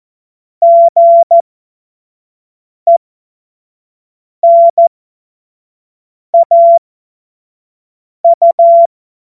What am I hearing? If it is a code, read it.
Morse code, heard as a keyed tone, about 14 wpm, spaced as for 5 wpm, 680 Hz. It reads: GENAU
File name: Morsekurs / Witze / Witz 14. März 2023 25 F70